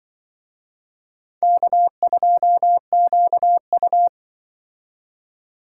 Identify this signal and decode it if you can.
Morse code, heard as a keyed tone, about 24 wpm, 700 Hz. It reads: K2QU
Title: Speed Racing - USA 1x2 Call Signs 20wpm